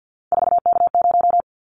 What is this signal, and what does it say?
Morse code, heard as a keyed tone, about 50 wpm, 700 Hz. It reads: 4K0
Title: Callsign Prefixes for Top 100 DXCC - 50wpm